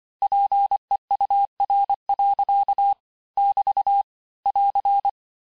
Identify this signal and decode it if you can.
Morse code, heard as a keyed tone, about 25 wpm, 800 Hz. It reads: PEUR. <BT> <AR>
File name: cours-868-teste-clair-vitesse-23mpm